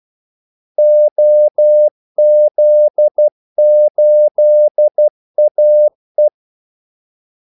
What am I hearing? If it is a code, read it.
Morse code, heard as a keyed tone, about 12 wpm, 600 Hz. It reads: OZ8AE
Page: 2 ekstra ordmellemrum mellem kaldesignalerne